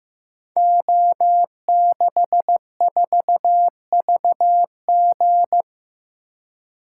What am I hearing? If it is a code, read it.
Morse code, heard as a keyed tone, about 15 wpm, 700 Hz. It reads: O64VG
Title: Code Groups - 5 Letter:Numbers - 15wpm